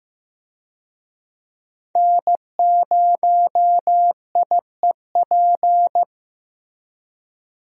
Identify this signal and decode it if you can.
Morse code, heard as a keyed tone, about 15 wpm, 700 Hz. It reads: N0IEP
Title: USA Call Signs - 5 Characters - Repeated 3x - 15wpm